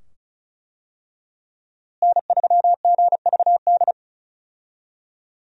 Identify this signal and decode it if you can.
Morse code, heard as a keyed tone, about 36 wpm, 700 Hz. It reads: N3GVD
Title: USA Call Signs - 5 Characters - Repeated 2x - 35wpm